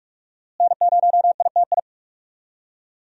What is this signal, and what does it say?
Morse code, heard as a keyed tone, about 45 wpm, 700 Hz. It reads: N0ITI
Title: USA Call Signs - 5 Characters - Repeated 2x - 45wpm